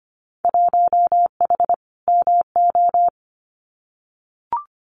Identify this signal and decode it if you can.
Morse code, heard as a keyed tone, about 25 wpm, 700 Hz. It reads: 1H MO
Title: Winter Field Day Exchanges 25wpm